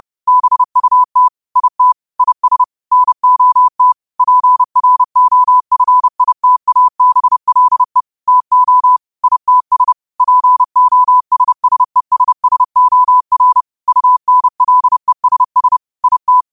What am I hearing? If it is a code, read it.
Morse code, heard as a keyed tone, about 30 wpm, 1000 Hz. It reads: BUT IT IS NOT PROFITABLE TO ITS POSSESSOR UNLESS IT
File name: qotdfiles / 30wpm